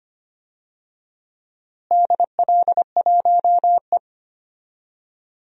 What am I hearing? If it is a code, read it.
Morse code, heard as a keyed tone, about 25 wpm, 700 Hz. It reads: DL1E
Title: DXCC-100 Call Signs - 4 Characters - Repeated 2x - 25wpm